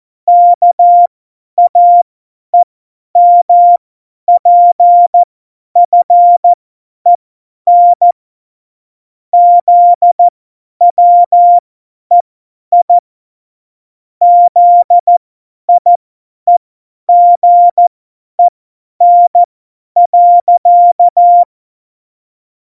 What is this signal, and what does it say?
Morse code, heard as a keyed tone, about 14 wpm, spaced as for 10 wpm, 700 Hz. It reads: KAEMPFEN ZWEI ZIEGEN.
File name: Morsekurs / Witze / Witz 17. September 2019, 50 F70